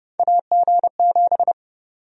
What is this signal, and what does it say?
Morse code, heard as a keyed tone, about 31 wpm, 700 Hz. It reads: AG7
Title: USA Call Sign Prefixes - 30wpm